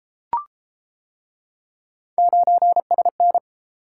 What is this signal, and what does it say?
Morse code, heard as a keyed tone, about 31 wpm, 700 Hz. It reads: 9SN